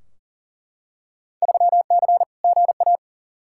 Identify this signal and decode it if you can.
Morse code, heard as a keyed tone, about 40 wpm, 700 Hz. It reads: 3C GA
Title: Field Day Exchanges Encore - 40wpm